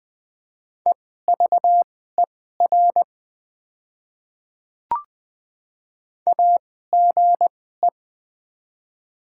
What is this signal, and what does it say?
Morse code, heard as a keyed tone, about 20 wpm, spaced as for 14 wpm, 690 Hz. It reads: EVER AGE
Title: 2x Character Spacing - Top 500 Words - 20wpm